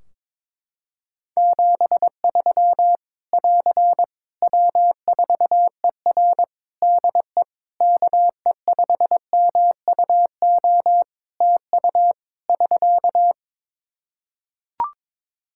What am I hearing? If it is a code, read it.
Morse code, heard as a keyed tone, about 22 wpm, 700 Hz. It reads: 73 <AR> W4ER DE KE5MUO TU <SK>
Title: K1USN SST QSO Practice 22wpm